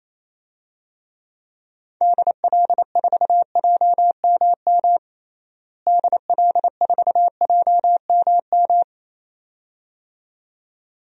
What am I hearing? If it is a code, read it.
Morse code, heard as a keyed tone, about 28 wpm, 700 Hz. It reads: DL4JMM DL4JMM
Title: DXCC-100 Call Signs - 6 Characters - Repeated 2x - 28wpm